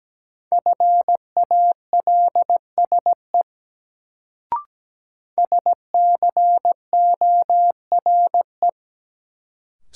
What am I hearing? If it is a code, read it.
Morse code, heard as a keyed tone, about 17 wpm, 700 Hz. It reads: FALSE SCORE
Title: Sentences from Top 1600 Words - New Word Drill Alt - 17wpm